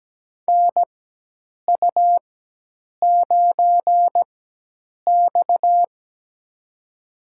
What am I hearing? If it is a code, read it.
Morse code, heard as a keyed tone, about 17 wpm, spaced as for 8 wpm, 700 Hz. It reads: NU9X
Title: USA Call Signs - 4x Character Spacing - 17wpm